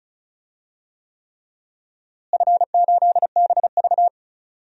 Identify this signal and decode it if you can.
Morse code, heard as a keyed tone, about 36 wpm, 700 Hz. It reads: F8BV
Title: DXCC-100 Call Signs - 4 Characters - Repeated 3x - 35wpm